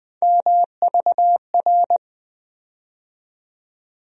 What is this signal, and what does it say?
Morse code, heard as a keyed tone, about 20 wpm, 700 Hz. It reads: MVR
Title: USA Call Sign Postfixes - 3 Letters 20wpm